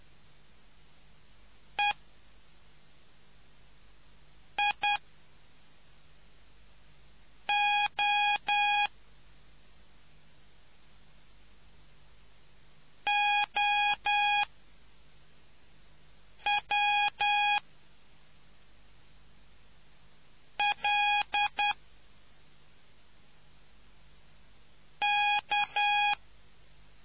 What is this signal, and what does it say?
Morse code, heard as a keyed tone, about 10 wpm, spaced as for 4 wpm, 820 Hz. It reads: E I O OW L K